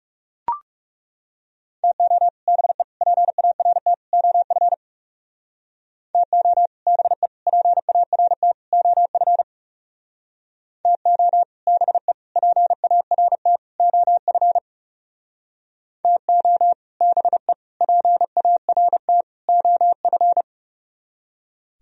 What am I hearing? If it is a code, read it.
Morse code, heard as a keyed tone, about 36 wpm, 700 Hz. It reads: TO BE PART OF TO BE PART OF TO BE PART OF TO BE PART OF
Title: Speed-Racing - Sets of 4 Words 30wpm